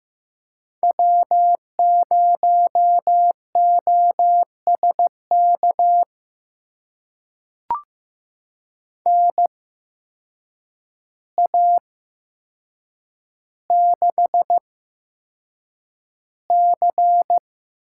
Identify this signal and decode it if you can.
Morse code, heard as a keyed tone, about 15 wpm, spaced as for 7 wpm, 700 Hz. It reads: W0OSK NA6C
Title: USA Call Signs - 8x Character Spacing - 15wpm